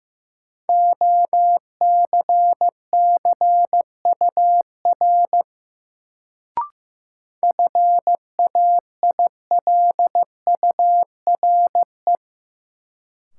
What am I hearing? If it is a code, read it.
Morse code, heard as a keyed tone, about 15 wpm, 700 Hz. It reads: OCCUR FAILURE
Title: Sentences from Top 900 Words - New Word Drill - 15wpm